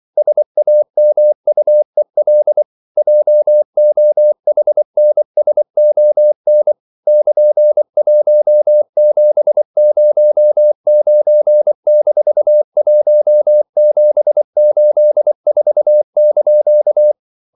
Morse code, heard as a keyed tone, about 24 wpm, 600 Hz. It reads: SAMUEL JOHNSON <KN>1709-1784)
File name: CW / Practice / EnglishQuotations0224